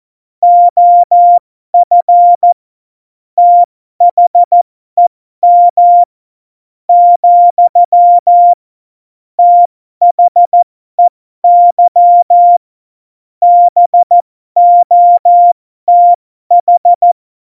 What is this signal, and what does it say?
Morse code, heard as a keyed tone, about 14 wpm, spaced as for 11 wpm, 700 Hz. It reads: OF THEM, THEY BOTH
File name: Morsekurs / Witze